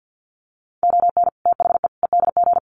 Morse code, heard as a keyed tone, about 50 wpm, 700 Hz. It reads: ON THE EDG
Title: Sets of 5 Words 50wpm